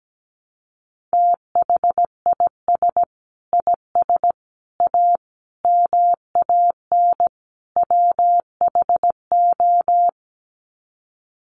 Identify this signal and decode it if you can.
Morse code, heard as a keyed tone, about 17 wpm, 700 Hz. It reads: THIS IS A MAN WHO
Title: Sets of 5 Words Encore2 17wpm